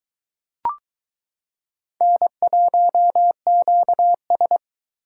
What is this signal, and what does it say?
Morse code, heard as a keyed tone, about 22 wpm, 700 Hz. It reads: N1QS